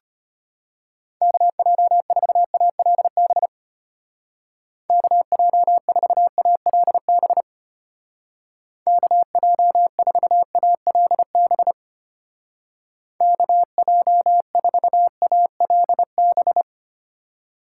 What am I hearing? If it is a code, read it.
Morse code, heard as a keyed tone, about 33 wpm, 700 Hz. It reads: KJ4ALB KJ4ALB KJ4ALB KJ4ALB